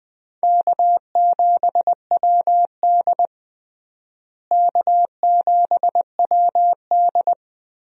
Morse code, heard as a keyed tone, about 20 wpm, 700 Hz. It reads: K7WD K7WD
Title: USA Call Signs - 4 Characters - Repeated 3x - 20wpm